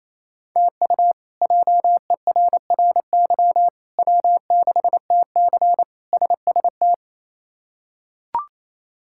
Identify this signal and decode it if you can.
Morse code, heard as a keyed tone, about 27 wpm, 700 Hz. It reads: TU JERRY W6TC SST
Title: K1USN SST QSO Practice - 28wpm